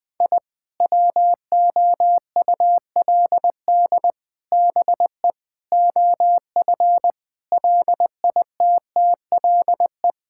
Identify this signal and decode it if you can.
Morse code, heard as a keyed tone, about 21 wpm, 700 Hz. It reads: I WOULD BE OF LITTLE
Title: Sentences from Top 1000 Words - Review Part 5 - 20wpm